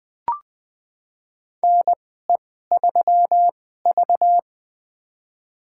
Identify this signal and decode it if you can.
Morse code, heard as a keyed tone, about 21 wpm, spaced as for 15 wpm, 700 Hz. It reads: NE3V
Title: USA Call Signs - 4 Characters - 2x Character Spacing - 20wpm